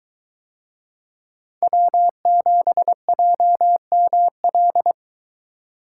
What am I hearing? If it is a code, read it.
Morse code, heard as a keyed tone, about 22 wpm, 700 Hz. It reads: W7JML